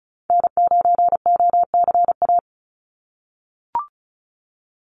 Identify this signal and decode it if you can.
Morse code, heard as a keyed tone, about 34 wpm, 700 Hz. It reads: N9OCA
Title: USA Call Signs 35wpm